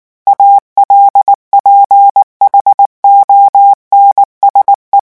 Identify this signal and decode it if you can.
Morse code, heard as a keyed tone, about 19 wpm, 800 Hz. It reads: ALPHONSE
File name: morse mp3 / revision9 citations humoristiques vitesse 18 mots minute